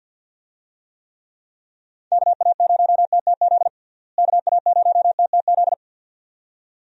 Morse code, heard as a keyed tone, about 46 wpm, 700 Hz. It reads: KA0TTZ KA0TTZ
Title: USA Call Signs - 6 Characters - Repeated 2x - 50wpm